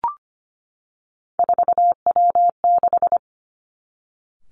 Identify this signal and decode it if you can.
Morse code, heard as a keyed tone, about 25 wpm, 700 Hz. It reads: 4W6
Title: Three Challenging Letter-Number Combinations 25wpm